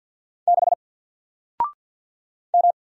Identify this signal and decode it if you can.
Morse code, heard as a keyed tone, about 54 wpm, 700 Hz. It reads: 6 M
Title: Single Letter-Number Alt 50wpm